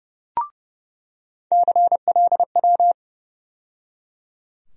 Morse code, heard as a keyed tone, about 30 wpm, 700 Hz. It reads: CLW
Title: USA Call Sign Postfixes - 3 Letters 30wpm